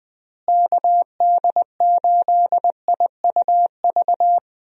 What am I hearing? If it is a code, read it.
Morse code, heard as a keyed tone, about 20 wpm, 700 Hz. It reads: KD8IUV
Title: USA Call Signs - 6 Characters - Repeated 2x - 20wpm